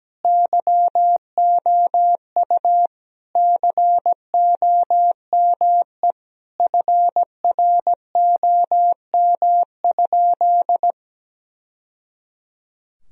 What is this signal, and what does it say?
Morse code, heard as a keyed tone, about 17 wpm, 700 Hz. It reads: YOU COME FROM?
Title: Sentences from Top 1000 Words - Review Part 2 - 17wpm